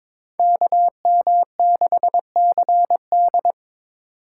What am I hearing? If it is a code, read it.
Morse code, heard as a keyed tone, about 26 wpm, 700 Hz. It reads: KM6CD
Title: Speed Racing - USA 2x2 Call Signs 22wpm